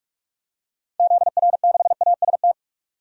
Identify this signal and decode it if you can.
Morse code, heard as a keyed tone, about 45 wpm, 700 Hz. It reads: ZR6AST
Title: DXCC-100 Call Signs - 6 Characters - 45wpm